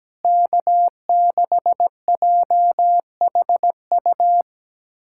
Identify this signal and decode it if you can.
Morse code, heard as a keyed tone, about 17 wpm, 700 Hz. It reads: K6JHU
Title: K1USN SST QSO Practice - 17wpm